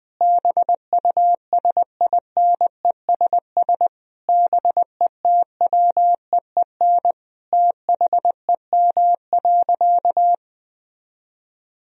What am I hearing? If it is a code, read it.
Morse code, heard as a keyed tone, about 21 wpm, 700 Hz. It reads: BUSINESS BETWEEN THEM.
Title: Sentences from Top 1000 Words - Review Part 1 - 20wpm